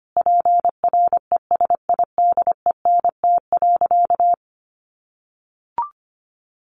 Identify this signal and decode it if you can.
Morse code, heard as a keyed tone, about 25 wpm, 700 Hz. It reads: PRESIDENT.
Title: Sentences from Top 1000 Words 25wpm